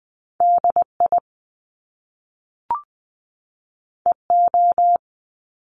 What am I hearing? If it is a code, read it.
Morse code, heard as a keyed tone, about 20 wpm, 700 Hz. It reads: DI EO
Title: Top Two Letter Combinations in Words 20wpm